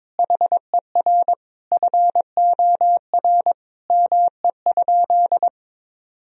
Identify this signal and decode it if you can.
Morse code, heard as a keyed tone, about 22 wpm, 700 Hz. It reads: HER FOR ME?
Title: Sentences from Top 1000 Words - Review Part 3 - 22wpm